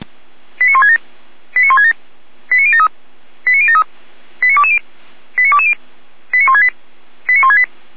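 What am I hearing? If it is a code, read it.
Five-tone selective call (ZVEI), heard as quick runs of five tones, tones of about 70 ms. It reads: 89167 89167 89073 89073 89220 89220 89168 89168
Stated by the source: ZVEI five-tone selective call received on an SDR.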